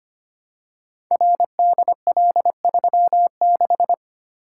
Morse code, heard as a keyed tone, about 25 wpm, 700 Hz. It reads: RDL36
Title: Code Groups - 5 Letter:Numbers - 25wpm